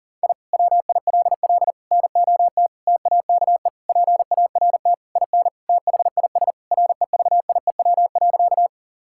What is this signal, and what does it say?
Morse code, heard as a keyed tone, about 40 wpm, 700 Hz. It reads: I WILL NOT TAKE PART IN THIS REVIEW.
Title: Sentences from Top 1000 Words - Review Part 6 - 40wpm